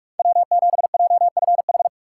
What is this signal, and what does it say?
Morse code, heard as a keyed tone, about 40 wpm, 700 Hz. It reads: W7JFH